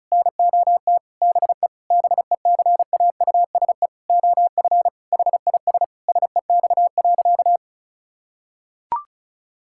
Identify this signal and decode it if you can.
Morse code, heard as a keyed tone, about 36 wpm, 700 Hz. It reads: NOT BE BECAUSE OF HIS SEX.
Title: Sentences from Top 1600 Words 35wpm